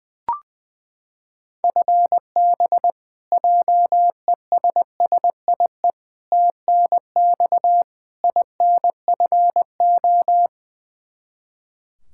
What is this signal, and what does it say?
Morse code, heard as a keyed tone, about 20 wpm, 700 Hz. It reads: FB JESSIE TNX INFO
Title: Practice QSO - Exchange 3 - More Info - 20wpm